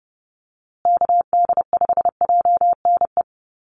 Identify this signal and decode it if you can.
Morse code, heard as a keyed tone, about 30 wpm, 700 Hz. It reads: KD5JNE
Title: USA Call Signs Repeated 3x 30wpm